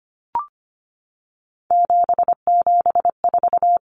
Morse code, heard as a keyed tone, about 25 wpm, 700 Hz. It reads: 774
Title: Common Cut Numbers - 3-Digit 25wpm